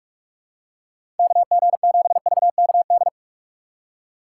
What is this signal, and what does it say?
Morse code, heard as a keyed tone, about 44 wpm, 700 Hz. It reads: KG7VKD
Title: USA Call Signs - 6 Characters - 6x Character Spacing - 45wpm